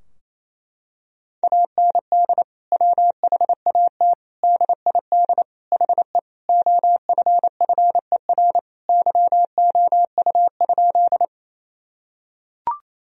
Morse code, heard as a keyed tone, about 28 wpm, 700 Hz. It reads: AND WHAT DID HE OFFER YOU?
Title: Sentences from Top 1000 Words - Review Part 4 - 28wpm